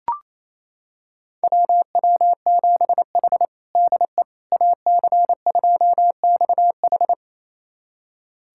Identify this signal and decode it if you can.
Morse code, heard as a keyed tone, about 28 wpm, 700 Hz. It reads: WW7H DE AC2XH
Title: USA Callsign DE Callsign 28wpm